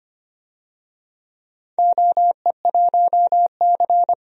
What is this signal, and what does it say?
Morse code, heard as a keyed tone, about 25 wpm, 700 Hz. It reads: OE1C
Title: DXCC-100 Call Signs - 4 Characters - Repeated 2x - 25wpm